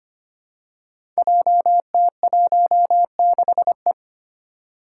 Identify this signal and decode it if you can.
Morse code, heard as a keyed tone, about 25 wpm, 700 Hz. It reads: JT16E
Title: Five Challenging Letter-Number Combinations 25wpm